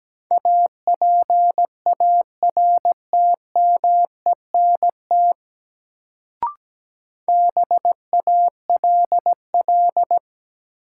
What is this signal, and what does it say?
Morse code, heard as a keyed tone, about 17 wpm, 700 Hz. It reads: APARTMENT BALL
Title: Sentences from Top 900 Words - New Word Drill - 17wpm